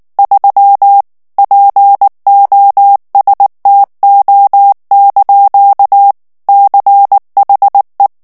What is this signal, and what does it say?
Morse code, heard as a keyed tone, about 19 wpm, 800 Hz. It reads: 3 POSTO) CHE
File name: morse mp3 / ITALIEN texte clair 2 vitesse 18 mots minute